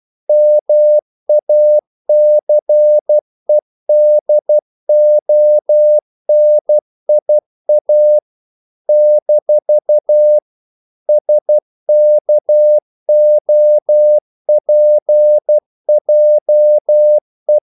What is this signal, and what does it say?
Morse code, heard as a keyed tone, about 12 wpm, 600 Hz. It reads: MACEDONIA - SKOPJE
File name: CW / Practice / NationalCapitals12